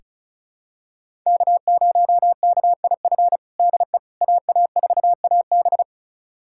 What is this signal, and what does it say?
Morse code, heard as a keyed tone, about 36 wpm, 700 Hz. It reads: K0KIF DE AA4AB
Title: USA Callsign DE Callsign - 5 Characters - 35wpm